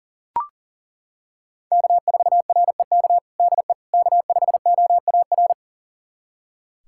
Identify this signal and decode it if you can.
Morse code, heard as a keyed tone, about 40 wpm, 700 Hz. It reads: K4REK DE K5OAR
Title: USA Callsign DE Callsign - 5 Characters - 40wpm